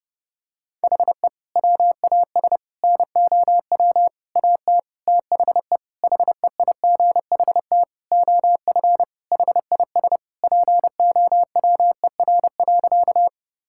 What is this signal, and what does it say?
Morse code, heard as a keyed tone, about 30 wpm, 700 Hz. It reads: HE WAS NOW AT THE HEIGHT OF HIS POWER.
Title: Sentences from Top 1600 Words 30wpm